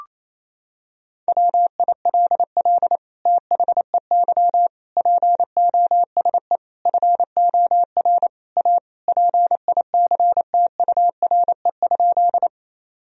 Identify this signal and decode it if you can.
Morse code, heard as a keyed tone, about 28 wpm, 700 Hz. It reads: WILL THEY POSE FOR A PICTURE?
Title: Sentences from Top 1000 Words - Review Part 2 - 28wpm